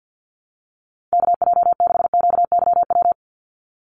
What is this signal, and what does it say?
Morse code, heard as a keyed tone, about 50 wpm, 700 Hz. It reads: KJ6QYW
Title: USA Call Signs 50wpm